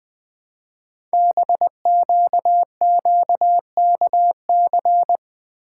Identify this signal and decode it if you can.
Morse code, heard as a keyed tone, about 20 wpm, 700 Hz. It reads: BQQKC
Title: Code Groups - 5 Letters - 20wpm